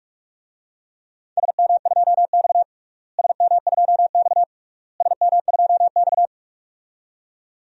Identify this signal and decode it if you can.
Morse code, heard as a keyed tone, about 48 wpm, 700 Hz. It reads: SM2X SM2X SM2X
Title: DXCC-100 Call Signs - 4 Characters - Repeated 3x - 45wpm